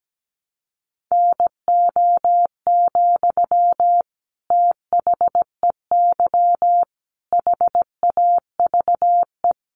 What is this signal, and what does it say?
Morse code, heard as a keyed tone, about 17 wpm, 700 Hz. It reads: NO, THEY HAVE
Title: Sentences from Top 1300 Words 17wpm